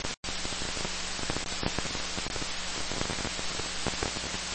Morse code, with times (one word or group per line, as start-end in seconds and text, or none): none